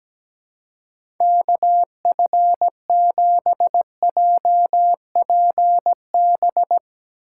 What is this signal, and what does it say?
Morse code, heard as a keyed tone, about 17 wpm, 700 Hz. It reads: KF7JPB
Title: USA Call Signs - 6 Characters - 2x Character Spacing - 17wpm